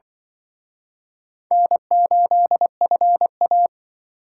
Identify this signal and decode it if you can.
Morse code, heard as a keyed tone, about 24 wpm, 700 Hz. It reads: N8FA
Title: Speed Racing - USA 1x2 Call Signs 20wpm